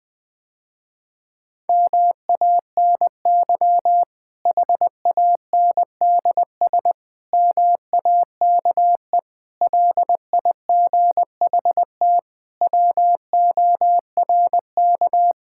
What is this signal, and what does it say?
Morse code, heard as a keyed tone, about 19 wpm, 700 Hz. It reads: MANY HANDS MAKE LIGHT WORK
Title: Difficult Sentences with Hints - 20wpm